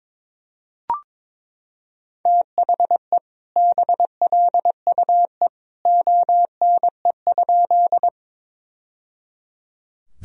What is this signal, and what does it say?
Morse code, heard as a keyed tone, about 22 wpm, 700 Hz. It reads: THE BLUE ONE?
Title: Sentences from Top 1000 Words - Review Part 3 - 22wpm